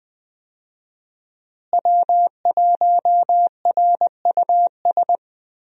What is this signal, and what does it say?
Morse code, heard as a keyed tone, about 20 wpm, 700 Hz. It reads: W1RUS
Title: USA Call Signs - 5 Characters - Repeated 2x - 20wpm